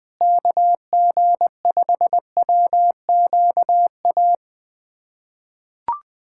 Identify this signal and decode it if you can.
Morse code, heard as a keyed tone, about 20 wpm, 700 Hz. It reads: KG5WQA
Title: USA Callsign DE Callsign - 6 Characters - 20wpm